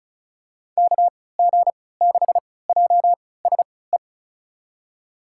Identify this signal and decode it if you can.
Morse code, heard as a keyed tone, about 36 wpm, spaced as for 20 wpm, 700 Hz. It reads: KG6JSE
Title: USA Call Signs - 6 Characters - 3x Character Spacing - 35wpm